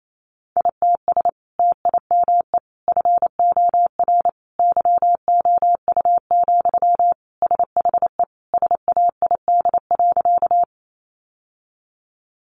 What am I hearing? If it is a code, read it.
Morse code, heard as a keyed tone, about 28 wpm, 700 Hz. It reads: ITS TIME FOR YOU, SHE SAID.
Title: Sentences from Top 600 Words 28wpm